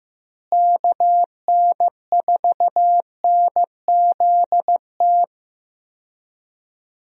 Morse code, heard as a keyed tone, about 15 wpm, 700 Hz. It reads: KN4NZT
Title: USA Call Signs - 6 Characters - Repeated 3x - 15wpm